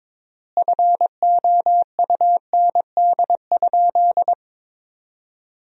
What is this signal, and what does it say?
Morse code, heard as a keyed tone, about 22 wpm, 700 Hz. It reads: FOUND?
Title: 3x Word-Spacing - Sentences from Top 300 Words - 22wpm